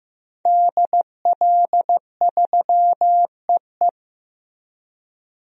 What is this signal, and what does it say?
Morse code, heard as a keyed tone, about 15 wpm, 700 Hz. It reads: DL3EE
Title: DXCC-100 Call Signs - 5 Characters - Repeated 3x - 15wpm